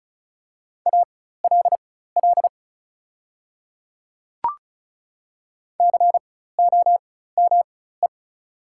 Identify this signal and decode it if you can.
Morse code, heard as a keyed tone, about 35 wpm, spaced as for 16 wpm, 700 Hz. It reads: ALL COME
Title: 4x Character Spacing - Top 100 Words - 35wpm